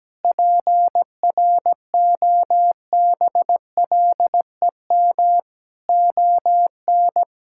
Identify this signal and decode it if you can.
Morse code, heard as a keyed tone, about 17 wpm, 700 Hz. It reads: PROBLEM ON
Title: Sentences from Top 1000 Words - Review Part 6 - 17wpm